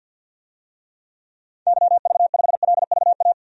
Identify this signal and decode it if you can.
Morse code, heard as a keyed tone, about 51 wpm, 700 Hz. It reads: YV5LVA
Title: DXCC-100 Call Signs - 6 Characters - 50wpm